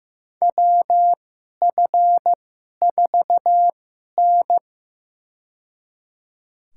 Morse code, heard as a keyed tone, about 15 wpm, spaced as for 11 wpm, 700 Hz. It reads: WF4N
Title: USA Call Signs - 4 Characters - 2x Character Spacing - 15wpm